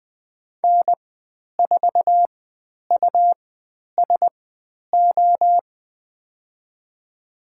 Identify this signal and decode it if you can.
Morse code, heard as a keyed tone, about 20 wpm, spaced as for 10 wpm, 700 Hz. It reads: N4USO
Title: 109 Lesson 18g - Letter m - Call Signs 20wpm